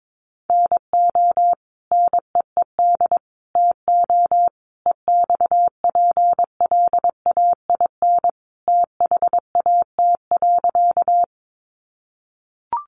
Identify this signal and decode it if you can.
Morse code, heard as a keyed tone, about 22 wpm, 700 Hz. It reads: NO NEED TO EXPLAIN THAT.
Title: Sentences from Top 1100 Words 22wpm